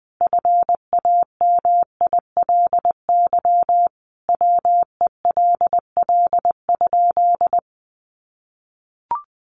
Morse code, heard as a keyed tone, about 20 wpm, 700 Hz. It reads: FAMILY WELL?
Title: Sentences from Top 400 Words 20wpm